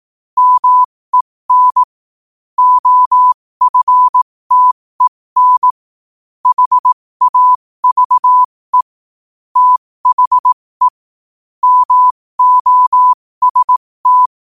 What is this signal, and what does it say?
Morse code, heard as a keyed tone, about 18 wpm, spaced as for 15 wpm, 1000 Hz. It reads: MEN OFTEN HAVE THE MOST
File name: qotdfiles / 15wpm